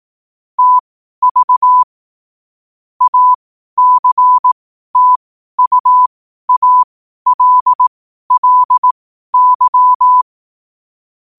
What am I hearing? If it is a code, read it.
Morse code, heard as a keyed tone, about 18 wpm, spaced as for 12 wpm, 1000 Hz. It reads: TV ACTUALLY
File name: qotdfiles / 12wpm